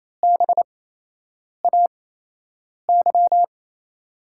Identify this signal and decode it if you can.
Morse code, heard as a keyed tone, about 28 wpm, spaced as for 8 wpm, 700 Hz. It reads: BAY
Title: USA Call Signs - 5 Characters - 8x Character Spacing - 28wpm